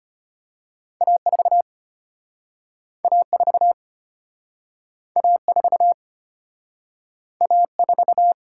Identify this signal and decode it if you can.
Morse code, heard as a keyed tone, about 34 wpm, 700 Hz. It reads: A4 A4 A4 A4